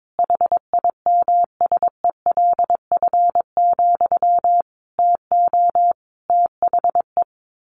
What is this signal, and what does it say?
Morse code, heard as a keyed tone, about 22 wpm, 700 Hz. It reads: HIMSELF, TO THE
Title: A Sound Of Thunder 22wpm